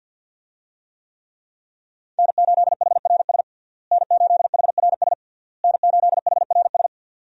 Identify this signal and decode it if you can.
Morse code, heard as a keyed tone, about 47 wpm, 700 Hz. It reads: N8HRS N8HRS N8HRS
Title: USA Call Signs - 5 Characters - Repeated 3x - 50wpm